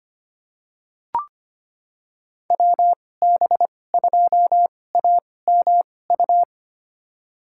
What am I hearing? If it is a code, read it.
Morse code, heard as a keyed tone, about 25 wpm, spaced as for 18 wpm, 700 Hz. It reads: WB2AMU
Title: USA Call Signs - 6 Characters - 2x Character Spacing - 25wpm